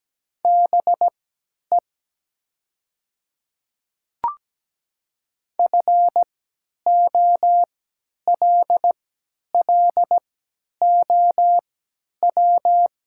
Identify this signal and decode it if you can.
Morse code, heard as a keyed tone, about 17 wpm, spaced as for 10 wpm, 700 Hz. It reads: BE FOLLOW
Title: 3x Character Spacing - Top 200 Words - 17wpm